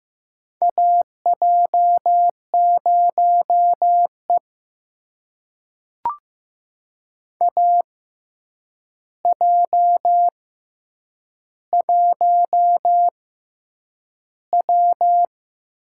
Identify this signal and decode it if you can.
Morse code, heard as a keyed tone, about 15 wpm, spaced as for 8 wpm, 700 Hz. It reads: AJ0E AJ1W
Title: USA Call Signs - 4 Characters - 6x Character Spacing - 15wpm